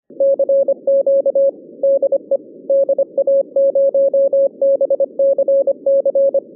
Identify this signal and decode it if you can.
Morse code, heard as a keyed tone, about 25 wpm, 550 Hz. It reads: CQ DE DA0BCC